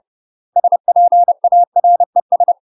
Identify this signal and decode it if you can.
Morse code, heard as a keyed tone, about 30 wpm, 700 Hz. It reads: SPARES